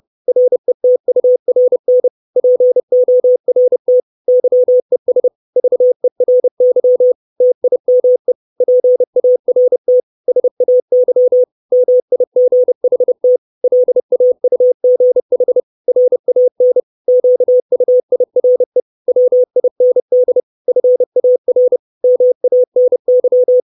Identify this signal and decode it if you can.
Morse code, heard as a keyed tone, about 30 wpm, 500 Hz. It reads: RETURN PORT YES VERY TIME PART SAY MIGHT LAUGH RAN QUIRE WIND FAR MANY
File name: 632 Words 30WPM